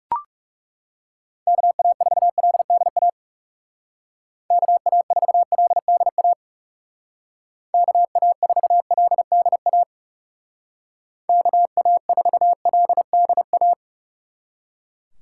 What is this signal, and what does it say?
Morse code, heard as a keyed tone, about 36 wpm, 700 Hz. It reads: KA4LDA KA4LDA KA4LDA KA4LDA
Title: Speed Racing - USA 2x3 Call Signs 30wpm